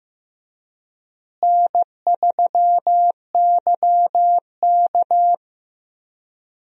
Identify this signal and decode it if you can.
Morse code, heard as a keyed tone, about 15 wpm, 700 Hz. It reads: N3YK
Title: USA Call Signs - 4 Characters - Repeated 2x - 15wpm